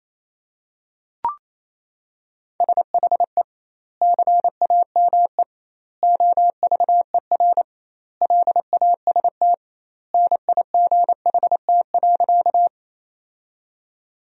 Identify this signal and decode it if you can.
Morse code, heard as a keyed tone, about 27 wpm, 700 Hz. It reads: SHE CAME OVER LAST NIGHT.
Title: 2x Word-Spacing - Sentences from Top 300 Words - 28wpm